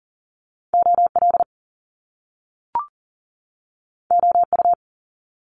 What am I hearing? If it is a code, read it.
Morse code, heard as a keyed tone, about 38 wpm, 700 Hz. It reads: OL OU
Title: Top Two Letter Combinations in Words 40wpm